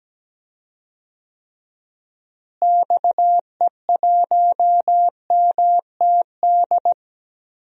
Morse code, heard as a keyed tone, about 17 wpm, 700 Hz. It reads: XE1MTD
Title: DXCC-100 Call Signs - 6 Characters - Repeated 2x - 17wpm